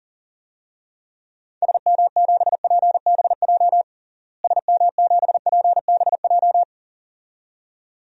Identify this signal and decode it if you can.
Morse code, heard as a keyed tone, about 40 wpm, 700 Hz. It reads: SM7PBJ SM7PBJ
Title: DXCC-100 Call Signs - 6 Characters - Repeated 2x - 40wpm